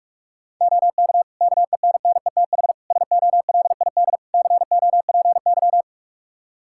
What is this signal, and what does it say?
Morse code, heard as a keyed tone, about 46 wpm, 700 Hz. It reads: OK KENNETH SOLID COPY
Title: Practice QSO - Exchange 3 - More Info - 45wpm